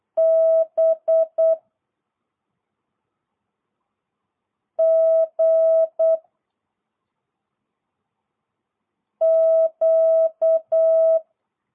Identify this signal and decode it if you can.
Morse code, heard as a keyed tone, about 8 wpm, 650 Hz. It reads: B G Q